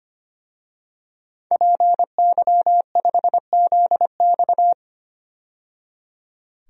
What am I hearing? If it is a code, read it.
Morse code, heard as a keyed tone, about 25 wpm, 700 Hz. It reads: PY5ZX
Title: DXCC-100 Call Signs - 5 Characters - 25wpm